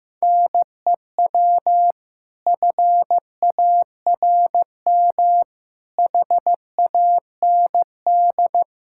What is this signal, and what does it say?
Morse code, heard as a keyed tone, about 15 wpm, 700 Hz. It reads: NEW FARM HAND
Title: Pangram Challenge v2 - 15wpm